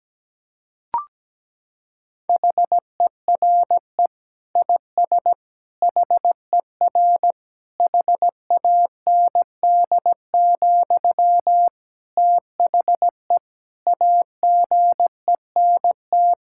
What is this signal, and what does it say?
Morse code, heard as a keyed tone, about 17 wpm, 700 Hz. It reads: HERE IS HER HAND, THE AGENT
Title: Sentences from Top 1600 Words 17wpm